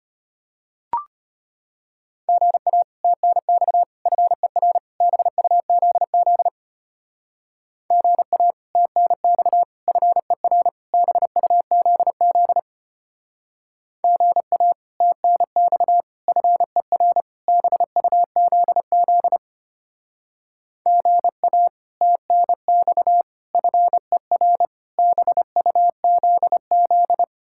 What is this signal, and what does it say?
Morse code, heard as a keyed tone, about 32 wpm, 700 Hz. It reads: GA TNX FER BUZZ GA TNX FER BUZZ GA TNX FER BUZZ GA TNX FER BUZZ